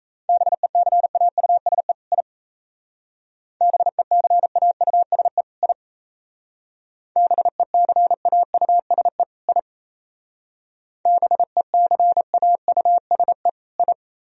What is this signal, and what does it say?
Morse code, heard as a keyed tone, about 34 wpm, 700 Hz. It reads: BECAUSE I BECAUSE I BECAUSE I BECAUSE I
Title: Speed-Racing - Sets of 2 Words 28wpm